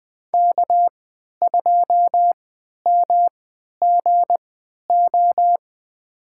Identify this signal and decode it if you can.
Morse code, heard as a keyed tone, about 20 wpm, spaced as for 11 wpm, 700 Hz. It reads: K2MGO
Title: USA Call Signs - 3x Character Spacing - 20wpm